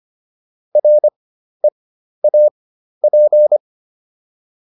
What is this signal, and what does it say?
Morse code, heard as a keyed tone, about 25 wpm, spaced as for 12 wpm, 600 Hz. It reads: REAP